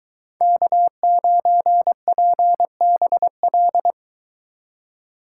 Morse code, heard as a keyed tone, about 20 wpm, 700 Hz. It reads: K9PBL